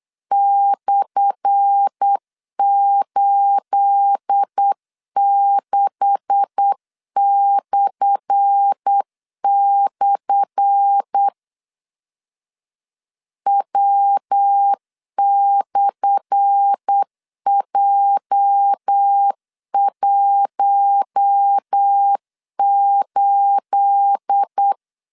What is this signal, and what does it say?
Morse code, heard as a keyed tone, about 8 wpm, 800 Hz. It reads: /86// W/J18
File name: LEZIONI / 10 SIGNS / 10A SLASH 8wpm 3452